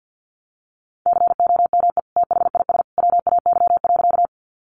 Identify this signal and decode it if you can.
Morse code, heard as a keyed tone, about 50 wpm, 700 Hz. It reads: COME THIS WAY.
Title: Sentences from Top 100 Words 50wpm